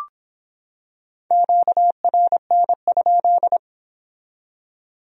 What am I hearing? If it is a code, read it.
Morse code, heard as a keyed tone, about 25 wpm, 700 Hz. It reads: QRN?